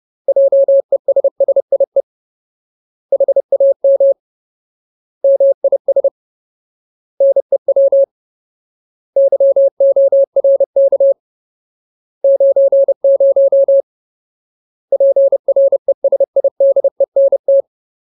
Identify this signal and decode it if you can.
Morse code, heard as a keyed tone, about 30 wpm, 550 Hz. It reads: JESSIE HAM MIS NEW YORK 90 PRESIDENT
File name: Ultimate Word List / 30WPM Ultimate